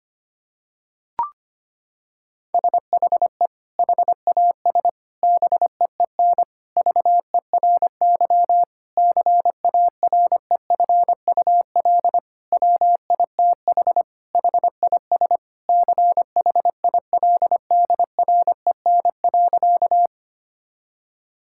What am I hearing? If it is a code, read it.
Morse code, heard as a keyed tone, about 25 wpm, 700 Hz. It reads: SHE HAS BEEN VERY CAREFUL WITH HIS CHILDREN.
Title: Sentences from Top 1600 Words 25wpm